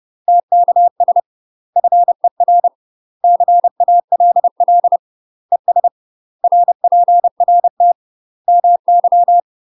Morse code, heard as a keyed tone, about 30 wpm, 700 Hz. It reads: TKS FER CALL ES RPRT MY
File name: UK MASTER QSOs TRAINING / 30WPM / UK MASTER QSOs 0030